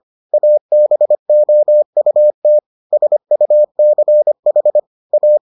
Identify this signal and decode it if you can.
Morse code, heard as a keyed tone, about 25 wpm, 600 Hz. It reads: ABOUT SUCH A